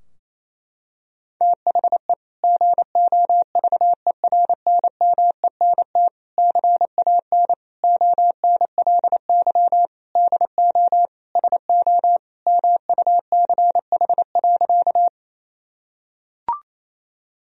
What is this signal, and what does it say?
Morse code, heard as a keyed tone, about 27 wpm, 700 Hz. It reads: THE GOVERNMENT CAN ONLY DO SO MUCH.
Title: Sentences from Top 1000 Words - Review Part 3 - 28wpm